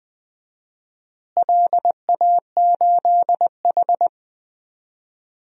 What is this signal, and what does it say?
Morse code, heard as a keyed tone, about 20 wpm, 700 Hz. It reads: LA8H
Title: DXCC-100 Call Signs - 4 Characters - Repeated 2x - 20wpm